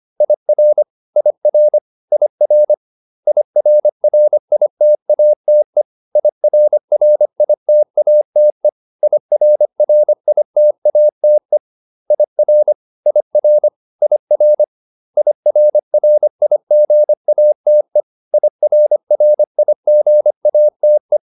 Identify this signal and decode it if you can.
Morse code, heard as a keyed tone, about 25 wpm, 600 Hz. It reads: IR IR IR IRRITATE IRRITATE IRRITATE IR IR IR IRRIGATE IRRIGATE